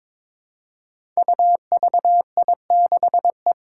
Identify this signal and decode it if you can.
Morse code, heard as a keyed tone, about 22 wpm, 700 Hz. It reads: UVI6E
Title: Code Groups - 5 Letter:Numbers - 22wpm